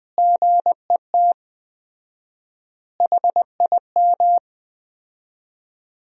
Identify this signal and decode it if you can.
Morse code, heard as a keyed tone, about 20 wpm, 700 Hz. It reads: GET HIM
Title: 4x Word-Spacing - Sentences from Top 400 Words - 20wpm